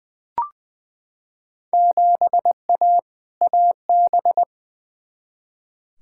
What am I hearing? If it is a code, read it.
Morse code, heard as a keyed tone, about 20 wpm, 700 Hz. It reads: 7A AB
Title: Field Day Exchanges Encore - 20wpm